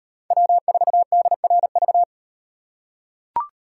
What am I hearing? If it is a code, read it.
Morse code, heard as a keyed tone, about 37 wpm, 700 Hz. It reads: W4DRV